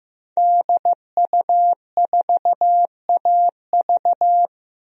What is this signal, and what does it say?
Morse code, heard as a keyed tone, about 15 wpm, 700 Hz. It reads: DU4AV
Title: DXCC-100 Call Signs - 5 Characters - Repeated 2x - 15wpm